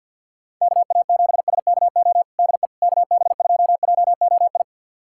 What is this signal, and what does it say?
Morse code, heard as a keyed tone, about 47 wpm, 700 Hz. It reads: KA7SKO DE KB2JOI
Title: USA Callsign DE Callsign - 6 Characters - 50wpm